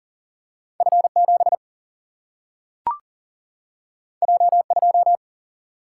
Morse code, heard as a keyed tone, about 40 wpm, 700 Hz. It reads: F7 J2
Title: Single-Letter Single-Number 40wpm